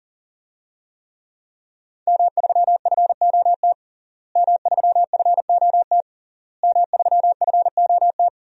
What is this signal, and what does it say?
Morse code, heard as a keyed tone, about 40 wpm, 700 Hz. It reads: M3FOT M3FOT M3FOT
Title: DXCC-100 Call Signs - 5 Characters - Repeated 3x - 40wpm